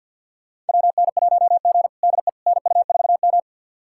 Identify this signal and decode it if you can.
Morse code, heard as a keyed tone, about 51 wpm, 700 Hz. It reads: WN1G DE NU4M
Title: USA Callsign DE Callsign - 4 Characters - 50wpm